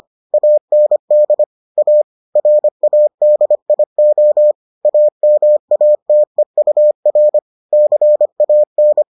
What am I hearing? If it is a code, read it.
Morse code, heard as a keyed tone, about 25 wpm, 600 Hz. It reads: AND A RADIO AMATEUR CAN